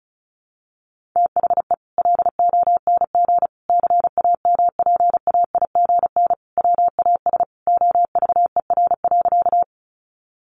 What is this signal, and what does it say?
Morse code, heard as a keyed tone, about 34 wpm, 700 Hz. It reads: THE LONG CAMPAIGN WAS OVER.
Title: Sentences from Top 1200 Words 35wpm